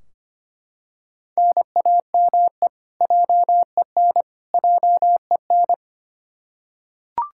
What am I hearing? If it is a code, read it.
Morse code, heard as a keyed tone, about 25 wpm, 700 Hz. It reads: NAME JEN JEN
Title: Practice QSO - Exchange 1 - Beginning - 25wpm